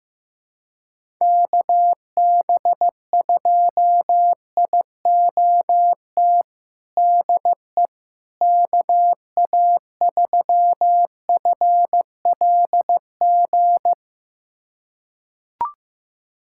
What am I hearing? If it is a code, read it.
Morse code, heard as a keyed tone, about 15 wpm, 700 Hz. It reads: KB2IOT DE KA3FLG
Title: USA Callsign DE Callsign - 6 Characters - 15wpm